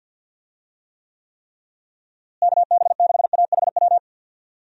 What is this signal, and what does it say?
Morse code, heard as a keyed tone, about 51 wpm, 700 Hz. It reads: KB6AHW
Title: USA Call Signs - 6 Characters - Repeated 3x - 50wpm